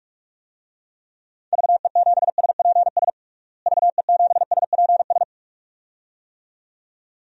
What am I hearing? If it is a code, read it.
Morse code, heard as a keyed tone, about 42 wpm, 700 Hz. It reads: VE7SPS VE7SPS
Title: DXCC-100 Call Signs - 6 Characters - Repeated 2x - 45wpm